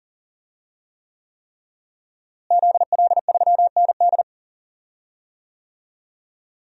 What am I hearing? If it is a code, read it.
Morse code, heard as a keyed tone, about 40 wpm, 700 Hz. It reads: ZL3ND
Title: DXCC-100 Call Signs - 5 Characters - Repeated 3x - 40wpm